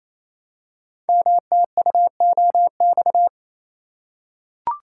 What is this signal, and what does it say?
Morse code, heard as a keyed tone, about 29 wpm, 700 Hz. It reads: MTUOX
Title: Code Groups - 5 Letters - 28wpm